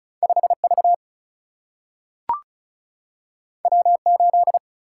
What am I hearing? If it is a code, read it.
Morse code, heard as a keyed tone, about 36 wpm, 700 Hz. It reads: 5 V W 8
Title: Single-Letter Single-Number 35wpm